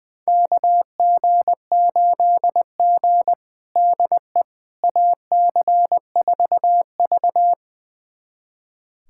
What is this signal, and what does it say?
Morse code, heard as a keyed tone, about 19 wpm, 700 Hz. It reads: KG8G DE AC4V
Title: USA Callsign DE Callsign - 4 Characters - 20wpm